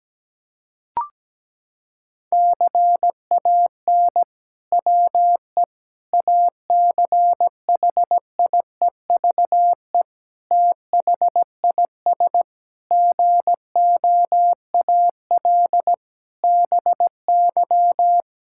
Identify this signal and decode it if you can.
Morse code, heard as a keyed tone, about 17 wpm, 700 Hz. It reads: CAN WE ACHIEVE THIS GOAL BY
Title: Sentences from Top 1000 Words - Review Part 5 - 17wpm